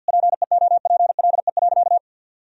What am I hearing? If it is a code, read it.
Morse code, heard as a keyed tone, about 50 wpm, 700 Hz. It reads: PEOPLE.
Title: Sentences from Top 1000 Words - Review Part 3 - 50wpm